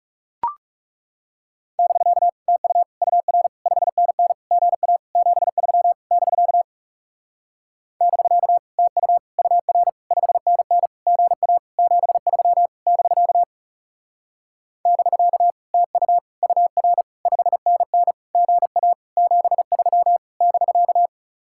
Morse code, heard as a keyed tone, about 39 wpm, 700 Hz. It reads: <BK> TU UR 5NN GA 73 <BK> <BK> TU UR 5NN GA 73 <BK> <BK> TU UR 5NN GA 73 <BK>